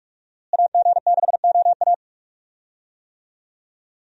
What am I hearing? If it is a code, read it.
Morse code, heard as a keyed tone, about 41 wpm, 700 Hz. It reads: AG6OA
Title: USA Call Signs - 5 Characters - Repeated 2x - 45wpm